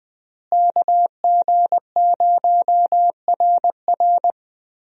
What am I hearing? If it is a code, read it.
Morse code, heard as a keyed tone, about 20 wpm, 700 Hz. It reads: KG0RR
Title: USA Call Signs - 5 Characters - Repeated 3x - 20wpm